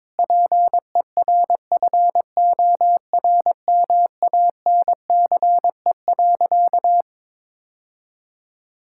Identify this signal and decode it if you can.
Morse code, heard as a keyed tone, about 22 wpm, 700 Hz. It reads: PERFORMANCE.
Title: Sentences from Top 1600 Words 22wpm